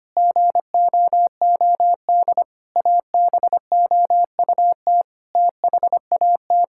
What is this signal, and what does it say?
Morse code, heard as a keyed tone, about 25 wpm, 700 Hz. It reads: GOOD ABOUT THAT
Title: Sentences from Top 1000 Words - Review Part 5 - 25wpm